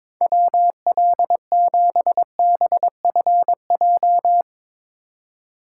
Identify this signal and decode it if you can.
Morse code, heard as a keyed tone, about 22 wpm, 700 Hz. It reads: WL7BFJ
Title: Speed Racing - USA 2x3 Call Signs 22wpm